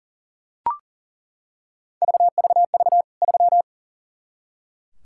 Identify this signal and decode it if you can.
Morse code, heard as a keyed tone, about 37 wpm, 700 Hz. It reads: VVV 3
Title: K1USN SST QSO Practice 40wpm